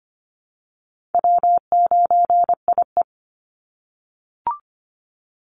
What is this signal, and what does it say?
Morse code, heard as a keyed tone, about 25 wpm, 700 Hz. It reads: W9IE
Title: Easy Call Signs - 25wpm